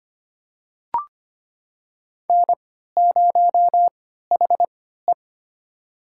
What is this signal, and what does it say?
Morse code, heard as a keyed tone, about 25 wpm, spaced as for 14 wpm, 700 Hz. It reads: N0HE
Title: USA Call Signs - 4 Characters - 3x Character Spacing - 25wpm